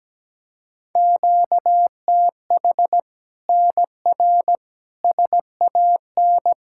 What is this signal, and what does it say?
Morse code, heard as a keyed tone, about 17 wpm, 700 Hz. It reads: QTH NR SAN
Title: Practice QSO - Exchange 1 - Beginning - 17wpm